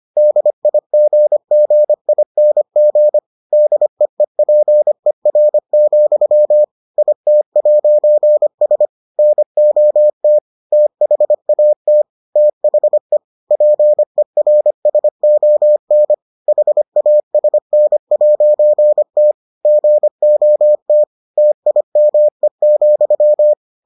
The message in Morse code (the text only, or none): DIGGING DEEPER, IT'S NOT THAT THE PERSON HASN'T GOT TIME,